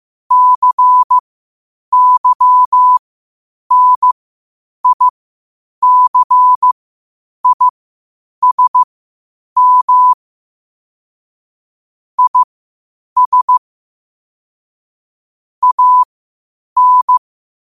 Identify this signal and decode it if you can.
Morse code, heard as a keyed tone, about 15 wpm, spaced as for 8 wpm, 1000 Hz. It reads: CYNICISM IS AN
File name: qotdfiles / 8wpm